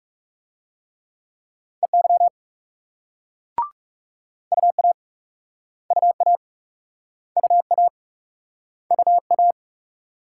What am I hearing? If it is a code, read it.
Morse code, heard as a keyed tone, about 42 wpm, 700 Hz. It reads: EY UA UA UA UA